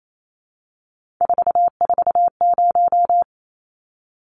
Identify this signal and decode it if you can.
Morse code, heard as a keyed tone, about 28 wpm, 700 Hz. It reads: 440
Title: Numbers - 3-Digit 28wpm